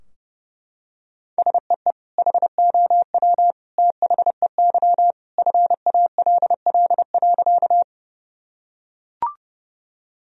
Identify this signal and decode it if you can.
Morse code, heard as a keyed tone, about 30 wpm, 700 Hz. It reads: SEE HOW THEY FALL.
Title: Sentences from Top 1000 Words - Review Part 2 - 30wpm